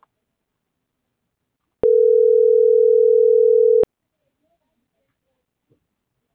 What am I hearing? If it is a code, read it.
Ringback tone (North America), 440 + 480 Hz.
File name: CENTRAL TERRESTRE / out karis - .77439